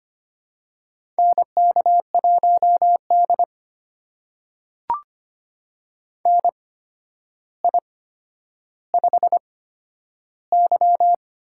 Morse code, heard as a keyed tone, about 25 wpm, spaced as for 11 wpm, 700 Hz. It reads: NK1D NI5Y
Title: USA Call Signs - 4 Characters - 8x Character Spacing - 25wpm